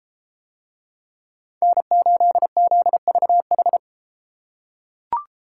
Morse code, heard as a keyed tone, about 31 wpm, 700 Hz. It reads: N8ZVH